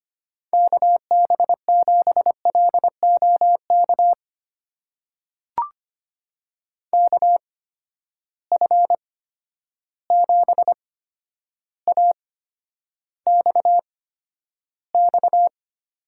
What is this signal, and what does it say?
Morse code, heard as a keyed tone, about 25 wpm, spaced as for 11 wpm, 700 Hz. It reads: KB7LOK KF7AXX
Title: USA Call Signs - 6 Characters - 8x Character Spacing - 25wpm